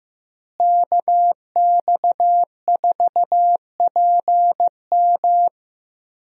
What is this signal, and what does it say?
Morse code, heard as a keyed tone, about 15 wpm, 700 Hz. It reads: KX4PM
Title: USA Call Signs - 5 Characters - Repeated 3x - 15wpm